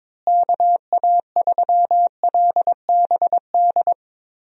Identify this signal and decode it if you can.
Morse code, heard as a keyed tone, about 22 wpm, 700 Hz. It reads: KA3LBD
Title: USA Call Signs - 6 Characters - Repeated 2x - 22wpm